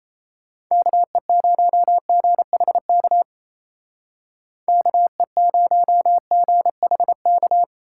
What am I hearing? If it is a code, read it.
Morse code, heard as a keyed tone, about 34 wpm, 700 Hz. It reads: KE0GHK KE0GHK
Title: Speed Racing - USA Call Signs 28wpm